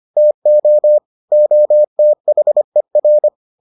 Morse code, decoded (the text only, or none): TO OTHER